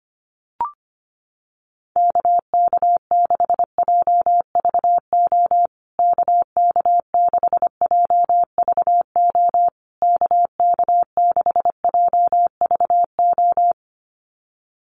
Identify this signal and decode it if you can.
Morse code, heard as a keyed tone, about 25 wpm, 700 Hz. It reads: KK6JVO KK6JVO KK6JVO
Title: USA Call Signs Repeated 3x 25wpm